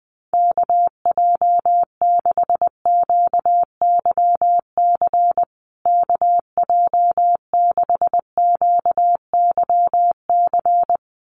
Morse code, heard as a keyed tone, about 20 wpm, 700 Hz. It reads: KJ6QYC KJ6QYC
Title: USA Call Signs Repeated 2x 20wpm